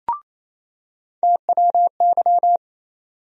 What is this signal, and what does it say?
Morse code, heard as a keyed tone, about 28 wpm, 700 Hz. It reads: TWY
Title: USA Call Sign Postfixes - 3 Letters 28wpm